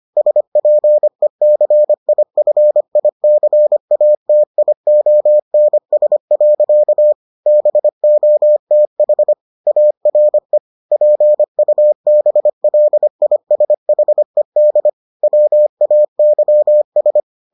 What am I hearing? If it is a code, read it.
Morse code, heard as a keyed tone, about 25 wpm, 600 Hz. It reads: SPECIFICATIONS. BOTH ARE PUBLISHED WAYS